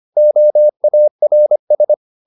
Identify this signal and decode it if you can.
Morse code, heard as a keyed tone, about 25 wpm, 600 Hz. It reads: OARS